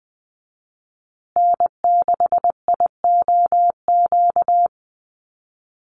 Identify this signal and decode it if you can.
Morse code, heard as a keyed tone, about 20 wpm, 700 Hz. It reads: N6IOQ
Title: USA Call Signs Repeated 3x 20wpm